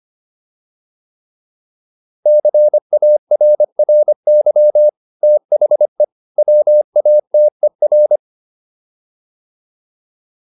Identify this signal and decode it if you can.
Morse code, heard as a keyed tone, about 25 wpm, 600 Hz. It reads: CARRY THE WATER
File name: PR403 25